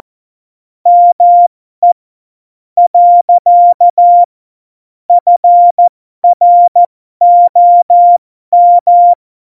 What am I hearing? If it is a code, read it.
Morse code, heard as a keyed tone, about 14 wpm, spaced as for 11 wpm, 700 Hz. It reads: ME. FROM
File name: Morsekurs / Witze